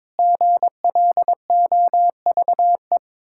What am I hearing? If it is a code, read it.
Morse code, heard as a keyed tone, about 22 wpm, 700 Hz. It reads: GLOVE
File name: Difficult Words with Hints - 22wpm